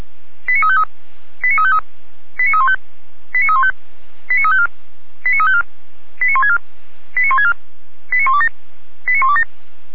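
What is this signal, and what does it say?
Five-tone selective call (ZVEI), heard as quick runs of five tones, tones of about 70 ms. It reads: 89353 89353 89315 89315 89354 89354 89164 89164 89127 89127
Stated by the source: Five-tone ZVEI selective call received on SDR, from audio.f85360.feuerwehr-orlamuende.org, a fire brigade (Feuerwehr Orlamünde) server.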